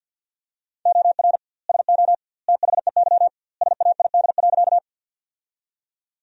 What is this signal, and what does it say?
Morse code, heard as a keyed tone, about 50 wpm, 700 Hz. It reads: OR SO THEY SAID.
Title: 2x Word-Spacing - Sentences from Top 100 Words - 50wpm